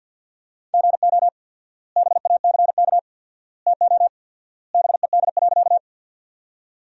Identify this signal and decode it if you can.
Morse code, heard as a keyed tone, about 51 wpm, 700 Hz. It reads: GO BACK TO BED.
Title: 4x Word-Spacing - Sentences from Top 500 Words - 50wpm